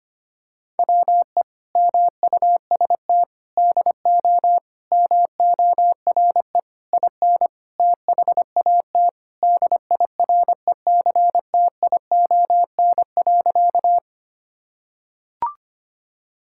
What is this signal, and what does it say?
Morse code, heard as a keyed tone, about 25 wpm, 700 Hz. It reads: WE MUST DO MORE IN THAT DIRECTION.
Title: Sentences from Top 1000 Words - Review Part 5 - 25wpm